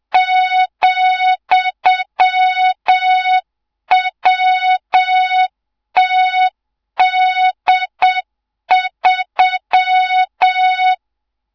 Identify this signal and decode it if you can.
Morse code, heard as a keyed tone, about 7 wpm, 3700 Hz, with noise. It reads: ,WTD3
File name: Morze-HA2MN / hang / 24 GYAKORLAT